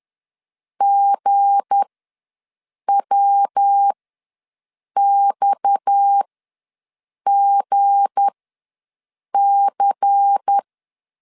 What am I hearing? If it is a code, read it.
Morse code, heard as a keyed tone, about 10 wpm, spaced as for 6 wpm, 800 Hz. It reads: GWXGC
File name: LEZIONI / 7 GF9Z / 7 G ws 10wpm 3843